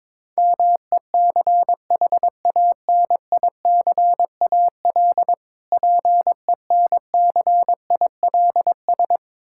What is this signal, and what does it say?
Morse code, heard as a keyed tone, about 22 wpm, 690 Hz. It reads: MECHANICAL PENCILS
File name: Difficult Words with Hints - 22wpm